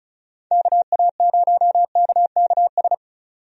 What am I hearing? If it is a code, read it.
Morse code, heard as a keyed tone, about 36 wpm, 700 Hz. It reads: KA0KKS
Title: USA Call Signs - 6 Characters - 35wpm